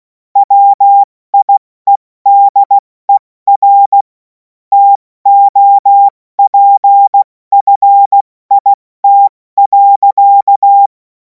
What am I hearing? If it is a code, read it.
Morse code, heard as a keyed tone, about 16 wpm, 800 Hz. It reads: WIEDER TOPFIT.
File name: Morsekurs / Witze / Witz 11. August 2026 70 F80